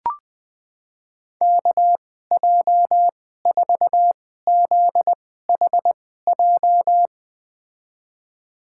Morse code, heard as a keyed tone, about 21 wpm, spaced as for 15 wpm, 700 Hz. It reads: KJ4ZHJ
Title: USA Call Signs - 6 Characters - 2x Character Spacing - 20wpm